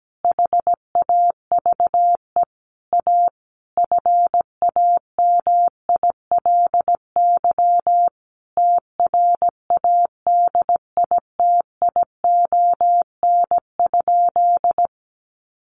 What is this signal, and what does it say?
Morse code, heard as a keyed tone, about 17 wpm, 700 Hz. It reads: HAVE A FAMILY TRADITION?
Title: Sentences from Top 1500 Words 17wpm